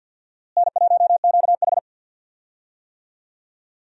Morse code, heard as a keyed tone, about 52 wpm, 700 Hz. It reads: N1QH
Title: USA Call Signs - 4 Characters - Repeated 3x - 50wpm